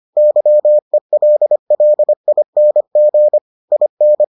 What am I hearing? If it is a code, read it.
Morse code, heard as a keyed tone, about 25 wpm, 600 Hz. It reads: YELLING IN